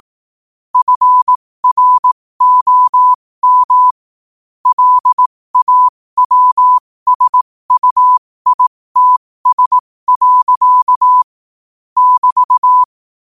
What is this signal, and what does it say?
Morse code, heard as a keyed tone, about 18 wpm, spaced as for 15 wpm, 1000 Hz. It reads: FROM LAWSUITS. <BT>
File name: qotdfiles / 15wpm